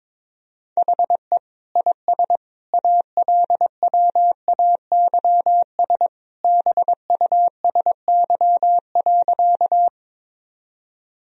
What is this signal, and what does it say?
Morse code, heard as a keyed tone, about 22 wpm, 700 Hz. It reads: HE IS ALWAYS BUSY.
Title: Sentences from Top 1000 Words - Review Part 3 - 22wpm